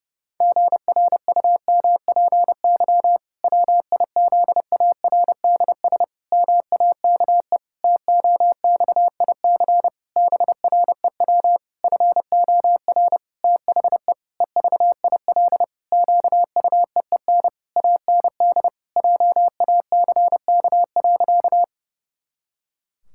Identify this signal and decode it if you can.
Morse code, heard as a keyed tone, about 32 wpm, 700 Hz. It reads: GRUMPY WIZARDS MAKE TOXIC BREW FOR THE EVIL QUEEN AND JACK.
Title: Pangram Challenge v2 - 30wpm